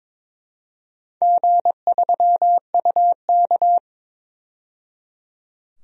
Morse code, heard as a keyed tone, about 22 wpm, 700 Hz. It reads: G3UK
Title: DXCC-100 Call Signs - 4 Characters - 22wpm